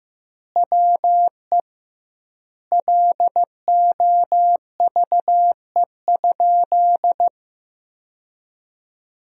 Morse code, heard as a keyed tone, about 15 wpm, 700 Hz. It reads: WE LOVE?
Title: 2x Word-Spacing - Sentences from Top 400 Words - 15wpm